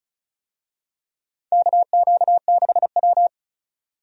Morse code, heard as a keyed tone, about 34 wpm, 700 Hz. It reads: KQ6W
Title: USA Call Signs - 4 Characters - 35wpm